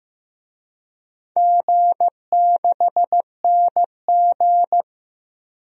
Morse code, heard as a keyed tone, about 15 wpm, 700 Hz. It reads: G6NG
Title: DXCC-100 Call Signs - 4 Characters - Repeated 2x - 15wpm